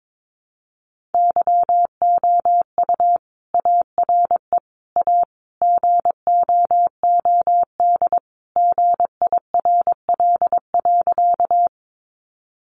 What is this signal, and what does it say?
Morse code, heard as a keyed tone, about 22 wpm, 700 Hz. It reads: YOU ARE A GOOD GIRL.
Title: Sentences from Top 400 Words 22wpm